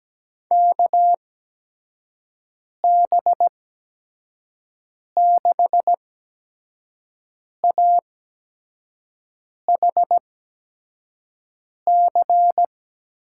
Morse code, heard as a keyed tone, about 17 wpm, spaced as for 5 wpm, 700 Hz. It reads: KB6AHC
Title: USA Call Signs - 6 Characters - 8x Character Spacing - 17wpm